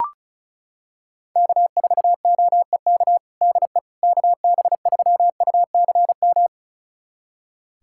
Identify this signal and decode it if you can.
Morse code, heard as a keyed tone, about 36 wpm, 690 Hz. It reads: K4OEK DE KB3UCM
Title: USA Callsign DE Callsign 35wpm